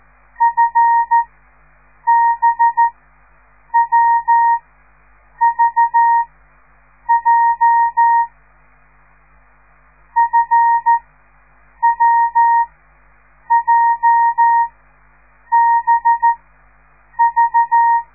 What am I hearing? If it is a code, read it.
Morse code, heard as a keyed tone, about 13 wpm, spaced as for 7 wpm, 930 Hz. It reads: FBWVJ FWJBV